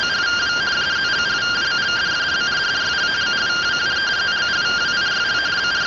Two-tone teleprinter signal, about 170 Hz shift.